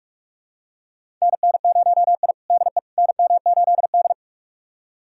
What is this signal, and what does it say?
Morse code, heard as a keyed tone, about 46 wpm, 700 Hz. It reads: NN0I DE NM8D
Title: USA Callsign DE Callsign - 4 Characters - 45wpm